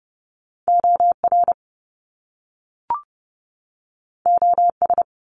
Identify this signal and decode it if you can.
Morse code, heard as a keyed tone, about 30 wpm, 700 Hz. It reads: OR OS
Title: Top Two Letter Combinations in Words 30wpm